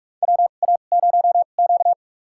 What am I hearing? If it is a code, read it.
Morse code, heard as a keyed tone, about 46 wpm, spaced as for 33 wpm, 700 Hz. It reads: WA0Q
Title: USA Call Signs - 4 Characters - 2x Character Spacing - 45wpm